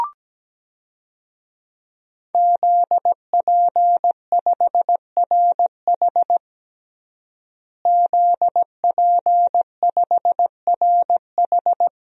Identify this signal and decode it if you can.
Morse code, heard as a keyed tone, about 17 wpm, 700 Hz. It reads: ZP5RH ZP5RH
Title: DXCC-100 Call Signs - 5 Characters - Repeated 3x - 17wpm